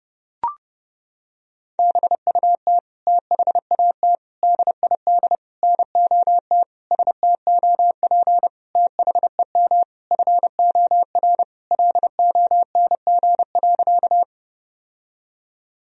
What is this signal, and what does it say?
Morse code, heard as a keyed tone, about 30 wpm, 700 Hz. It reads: BUT THAT DID NOT STOP THEM FOR LONG.
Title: Sentences from Top 1000 Words - Review Part 1 - 30wpm